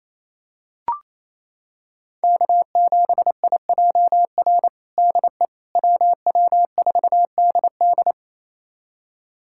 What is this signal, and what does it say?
Morse code, heard as a keyed tone, about 28 wpm, 700 Hz. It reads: K7IJR DE WW4DD
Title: USA Callsign DE Callsign - 5 Characters - 28wpm